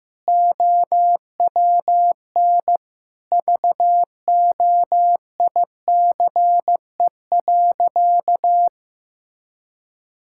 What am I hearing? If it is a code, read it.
Morse code, heard as a keyed tone, about 15 wpm, 700 Hz. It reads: OWN VOICE.
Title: Sentences from Top 1600 Words 15wpm